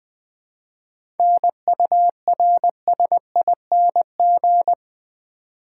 Morse code, heard as a keyed tone, about 20 wpm, 700 Hz. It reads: NURSING
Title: 7 Letter Words - First 3 Letters Given - 20wpm